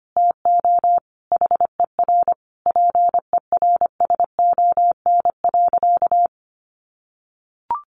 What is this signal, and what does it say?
Morse code, heard as a keyed tone, about 25 wpm, 700 Hz. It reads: TO HER PERSON.
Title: Sentences from Top 1400 Words 25wpm